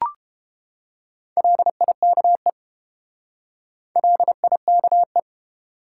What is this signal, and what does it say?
Morse code, heard as a keyed tone, about 31 wpm, 700 Hz. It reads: LIKE LIKE